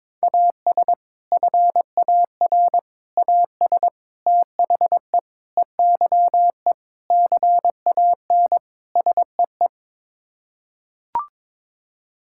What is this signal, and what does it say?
Morse code, heard as a keyed tone, about 22 wpm, 700 Hz. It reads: AS FAR AS THE EYE CAN SEE
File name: practice / 22wpm / Sets of 7 Words - 22wpm